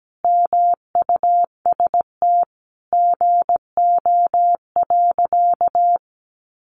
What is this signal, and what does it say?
Morse code, heard as a keyed tone, about 17 wpm, 700 Hz. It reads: MUST GO.
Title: Sentences from Top 1200 Words 17wpm